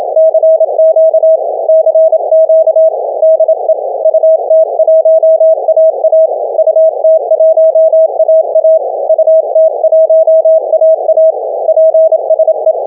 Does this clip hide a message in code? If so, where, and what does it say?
Yes: Morse code from 0.2 to 12.7 s, heard as a keyed tone, about 27 wpm, 640 Hz. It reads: CQ CQ DE UT1AA UT1AA UT1AA PSE